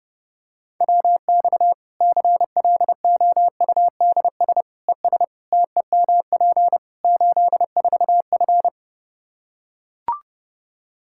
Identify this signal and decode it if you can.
Morse code, heard as a keyed tone, about 32 wpm, 700 Hz. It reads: WX CLOUDS ES TEMP 84F
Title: Practice QSO - Exchange 2 - Info - 30wpm